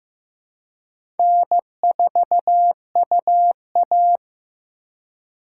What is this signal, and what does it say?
Morse code, heard as a keyed tone, about 15 wpm, 700 Hz. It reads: N4UA
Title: USA Call Signs - 2x Character Spacing - 15wpm